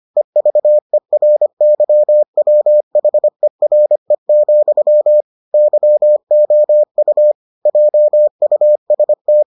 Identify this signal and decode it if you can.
Morse code, heard as a keyed tone, about 25 wpm, 600 Hz. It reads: EVERYWHERE, YOU JUST